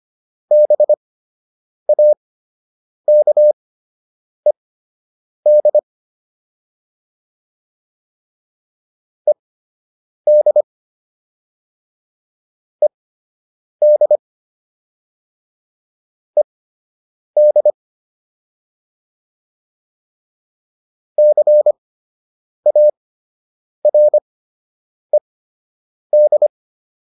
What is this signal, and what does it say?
Morse code, heard as a keyed tone, about 25 wpm, spaced as for 8 wpm, 600 Hz. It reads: BAKED ED ED ED CARED